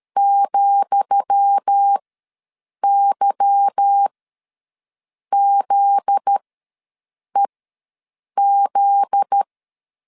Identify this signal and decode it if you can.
Morse code, heard as a keyed tone, about 13 wpm, spaced as for 6 wpm, 800 Hz. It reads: ,YZEZ
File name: LEZIONI / 10 SIGNS / 10D COMA ws 12wpm 3618